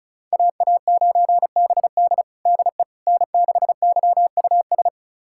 Morse code, heard as a keyed tone, about 36 wpm, 700 Hz. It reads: AA9BD DE N6YUS
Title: USA Callsign DE Callsign 35wpm